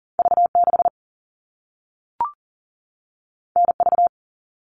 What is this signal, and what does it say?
Morse code, heard as a keyed tone, about 40 wpm, 700 Hz. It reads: V6 NV
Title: All Cut Numbers - 2-Digit 40wpm